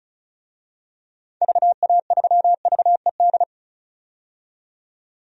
Morse code, heard as a keyed tone, about 36 wpm, 700 Hz. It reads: VA3VED
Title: DXCC-100 Call Signs - 6 Characters - 35wpm